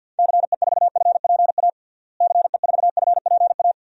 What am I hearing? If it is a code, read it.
Morse code, heard as a keyed tone, about 53 wpm, 700 Hz. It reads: CE4FPA CE4FPA
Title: DXCC-100 Call Signs - 6 Characters - Repeated 2x - 50wpm